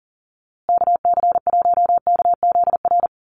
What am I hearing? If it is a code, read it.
Morse code, heard as a keyed tone, about 40 wpm, 700 Hz. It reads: KC1KZR
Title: USA Call Signs Repeated 2x 40wpm